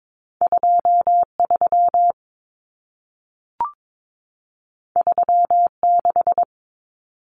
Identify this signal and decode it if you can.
Morse code, heard as a keyed tone, about 22 wpm, 700 Hz. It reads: 23 36
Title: Numbers - 2-Digit 22wpm